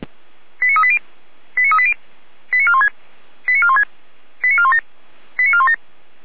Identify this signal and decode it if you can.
Five-tone selective call (ZVEI), heard as quick runs of five tones, tones of about 70 ms. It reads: 89380 89380 89416 89416 89417 89417